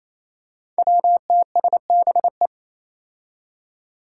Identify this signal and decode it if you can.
Morse code, heard as a keyed tone, about 28 wpm, 700 Hz. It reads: WTSBE
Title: Five Challenging Letter-Number Combinations 28wpm